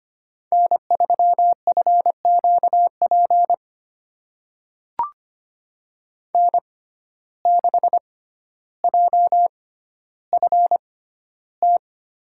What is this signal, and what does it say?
Morse code, heard as a keyed tone, about 25 wpm, spaced as for 13 wpm, 700 Hz. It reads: N3FQP N6JFT
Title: USA Call Signs - 5 Characters - 6x Character Spacing - 25wpm